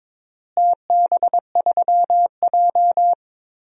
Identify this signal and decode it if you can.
Morse code, heard as a keyed tone, about 22 wpm, 700 Hz. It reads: TB3J
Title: DXCC-100 Call Signs - 4 Characters - Repeated 2x - 22wpm